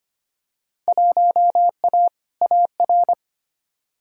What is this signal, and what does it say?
Morse code, heard as a keyed tone, about 25 wpm, 700 Hz. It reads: 1A AR
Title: Field Day Exchanges Encore - 25wpm